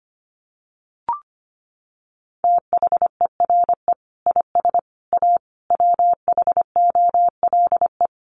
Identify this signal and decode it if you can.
Morse code, heard as a keyed tone, about 25 wpm, 700 Hz. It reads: THERE IS A WHOLE
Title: Sentences from Top 1400 Words 25wpm